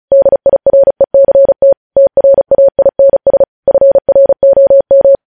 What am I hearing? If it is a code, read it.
Morse code, heard as a keyed tone, about 36 wpm, 550 Hz. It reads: DIRECT TRAINS FROM
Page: news headlines in morse code at 35 wpm